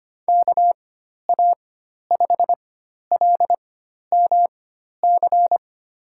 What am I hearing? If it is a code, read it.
Morse code, heard as a keyed tone, about 25 wpm, spaced as for 12 wpm, 700 Hz. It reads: KA5LMC
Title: USA Call Signs - 6 Characters - 4x Character Spacing - 25wpm